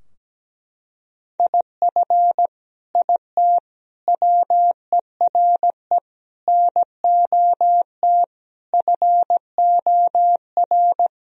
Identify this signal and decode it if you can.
Morse code, heard as a keyed tone, about 17 wpm, 700 Hz. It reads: IF IT WERE NOT FOR
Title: Sentences from Top 1000 Words - Review Part 2 - 17wpm